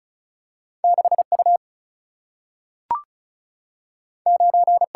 Morse code, heard as a keyed tone, about 36 wpm, 700 Hz. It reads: 6 U 9
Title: Single-Letter Single-Number 35wpm